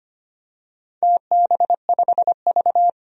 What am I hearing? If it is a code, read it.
Morse code, heard as a keyed tone, about 25 wpm, 700 Hz. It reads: TB5V
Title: DXCC-100 Call Signs - 4 Characters - 25wpm